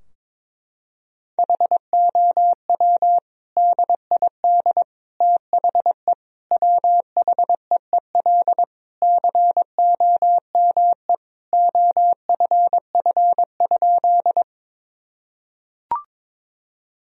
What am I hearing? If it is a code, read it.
Morse code, heard as a keyed tone, about 22 wpm, 700 Hz. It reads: HOW DID THE WHEEL COME OFF?
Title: Sentences from Top 1000 Words - Review Part 3 - 22wpm